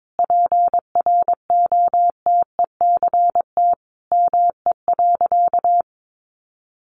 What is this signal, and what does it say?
Morse code, heard as a keyed tone, about 22 wpm, 700 Hz. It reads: PROTECT ME.
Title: Sentences from Top 1100 Words 22wpm